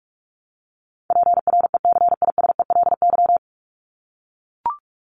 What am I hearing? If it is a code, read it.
Morse code, heard as a keyed tone, about 44 wpm, 700 Hz. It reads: PRECISELY
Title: LY Suffix - Top 100 Words with Base Words 45wpm